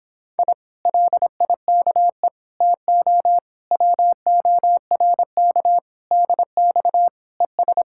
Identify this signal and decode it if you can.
Morse code, heard as a keyed tone, about 26 wpm, 700 Hz. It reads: I LIKE TO WORK DX ES
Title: Speed-Racing Common QSO Phrases - 22wpm